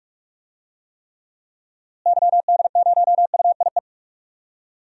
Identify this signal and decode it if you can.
Morse code, heard as a keyed tone, about 42 wpm, 700 Hz. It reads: YD0UIE
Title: DXCC-100 Call Signs - 6 Characters - Repeated 2x - 45wpm